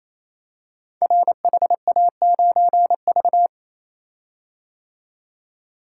Morse code, heard as a keyed tone, about 27 wpm, 700 Hz. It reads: RHA9V
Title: Code Groups - 5 Letter:Numbers - 28wpm